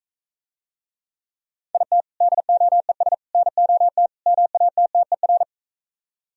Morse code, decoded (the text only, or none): IT DOES NOT MATTER